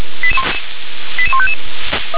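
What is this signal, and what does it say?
Five-tone selective call (ZVEI), heard as quick runs of five tones, tones of about 70 ms. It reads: 88155
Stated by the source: ZVEI five-tone selective call 88190, received via SDR.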